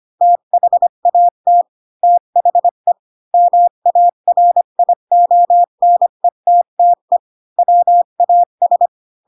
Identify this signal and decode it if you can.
Morse code, heard as a keyed tone, about 25 wpm, 700 Hz. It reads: THAT THE MARIONETTE WAS
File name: Pinocchio / Pinocchio ch7 20 25 ik1hge